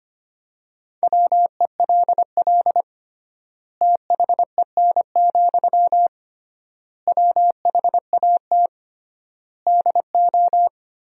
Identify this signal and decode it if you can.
Morse code, heard as a keyed tone, about 25 wpm, 700 Hz. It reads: WELL THEN, WHAT DO
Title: 3x Word-Spacing - Sentences from Top 200 Words - 25wpm